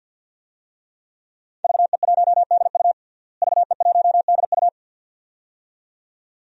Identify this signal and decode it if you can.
Morse code, heard as a keyed tone, about 51 wpm, 700 Hz. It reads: VE1DU VE1DU
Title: DXCC-100 Call Signs - 5 Characters - Repeated 2x - 50wpm